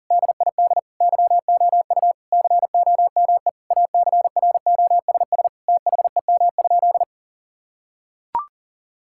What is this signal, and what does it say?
Morse code, heard as a keyed tone, about 40 wpm, 700 Hz. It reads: DID YOU COME ACROSS THEM?
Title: Sentences from Top 1000 Words - Review Part 5 - 40wpm